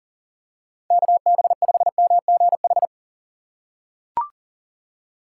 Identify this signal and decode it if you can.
Morse code, heard as a keyed tone, about 40 wpm, 700 Hz. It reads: KB5MGH
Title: USA Call Signs - 6 Characters - 2x Character Spacing - 40wpm